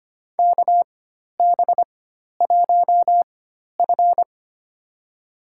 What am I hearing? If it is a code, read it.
Morse code, heard as a keyed tone, about 25 wpm, spaced as for 12 wpm, 700 Hz. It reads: KB1F
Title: USA Call Signs - 4 Characters - 4x Character Spacing - 25wpm